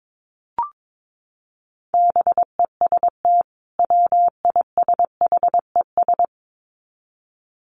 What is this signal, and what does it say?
Morse code, heard as a keyed tone, about 22 wpm, 700 Hz. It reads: BEST WISHES
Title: Common QSO Phrases 22wpm